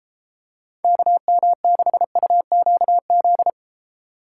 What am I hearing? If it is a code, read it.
Morse code, heard as a keyed tone, about 30 wpm, 700 Hz. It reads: KM6UQZ